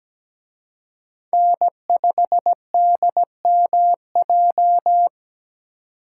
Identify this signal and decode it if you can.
Morse code, heard as a keyed tone, about 20 wpm, 700 Hz. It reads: N5DMJ
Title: Speed Racing - USA 1x3 Call Signs 17wpm